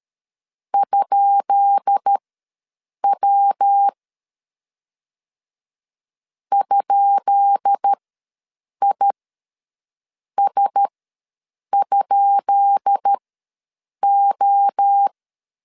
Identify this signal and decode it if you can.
Morse code, heard as a keyed tone, about 13 wpm, spaced as for 7 wpm, 800 Hz. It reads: ?W ?IS?O
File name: LEZIONI / 10 SIGNS / 10B INT ws 12wpm 3527